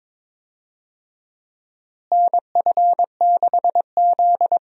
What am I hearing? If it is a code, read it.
Morse code, heard as a keyed tone, about 21 wpm, 700 Hz. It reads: NF6Z
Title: USA Call Signs - 4 Characters - Repeated 3x - 22wpm